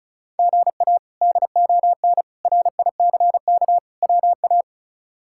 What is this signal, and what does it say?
Morse code, heard as a keyed tone, about 36 wpm, 700 Hz. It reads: GA DON RICK WA
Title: K1USN SST QSO Practice - 35wpm